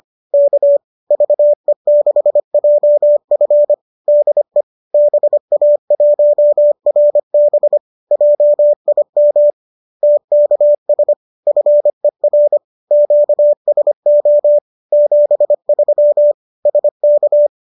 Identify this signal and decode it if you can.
Morse code, heard as a keyed tone, about 25 wpm, 590 Hz. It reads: K VE6JF DE BA1RB JIM TKS FER QSO 73 SK
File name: qso203 25